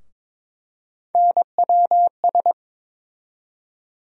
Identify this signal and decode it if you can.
Morse code, heard as a keyed tone, about 22 wpm, 700 Hz. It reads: NWS
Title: USA Call Sign Postfixes - 3 Letters 22wpm